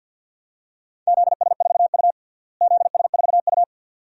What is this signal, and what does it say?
Morse code, heard as a keyed tone, about 51 wpm, 700 Hz. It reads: ZS4U ZS4U
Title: DXCC-100 Call Signs - 4 Characters - Repeated 2x - 50wpm